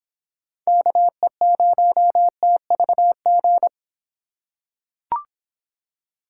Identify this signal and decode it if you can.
Morse code, heard as a keyed tone, about 26 wpm, 700 Hz. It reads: KE0TVG